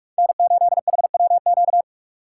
Morse code, heard as a keyed tone, about 58 wpm, 700 Hz. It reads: N8HWQ